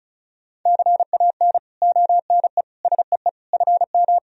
Morse code, heard as a keyed tone, about 36 wpm, 700 Hz. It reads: CAN ONE SEE FM
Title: Sentences from Top 100 Words and Abbreviations 35wpm